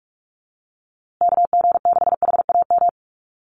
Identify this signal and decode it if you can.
Morse code, heard as a keyed tone, about 44 wpm, 700 Hz. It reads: KG6HAM
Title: USA Call Signs Repeated 2x 45wpm